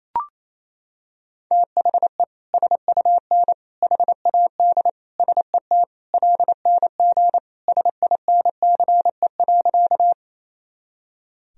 Morse code, heard as a keyed tone, about 28 wpm, 700 Hz. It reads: THE SUN HAD SET LNG SINCE.
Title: Sentences from Top 300 Words and Abbreviations 28wpm